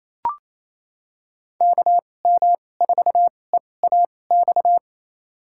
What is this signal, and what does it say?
Morse code, heard as a keyed tone, about 28 wpm, spaced as for 20 wpm, 700 Hz. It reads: KM4EAX
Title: USA Call Signs - 6 Characters - 2x Character Spacing - 28wpm